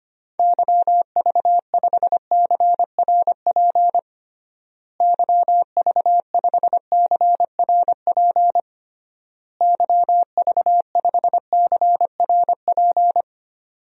Morse code, heard as a keyed tone, about 25 wpm, 700 Hz. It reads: YV5CRP YV5CRP YV5CRP
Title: DXCC-100 Call Signs - 6 Characters - Repeated 3x - 25wpm